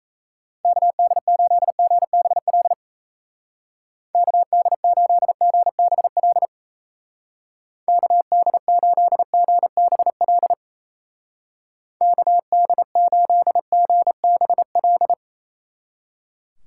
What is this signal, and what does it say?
Morse code, heard as a keyed tone, about 34 wpm, 700 Hz. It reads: KD8GBL KD8GBL KD8GBL KD8GBL
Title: Speed Racing - USA 2x3 Call Signs 28wpm